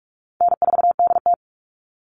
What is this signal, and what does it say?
Morse code, heard as a keyed tone, about 46 wpm, 700 Hz. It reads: N4DT
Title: Easy Call Signs - 45wpm